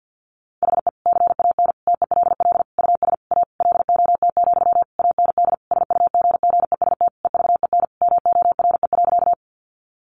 Morse code, heard as a keyed tone, about 50 wpm, 700 Hz. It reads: HE CAN TELL US A LOT, AND SUGGEST EVEN MORE.
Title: Sentences from Top 1300 Words 50wpm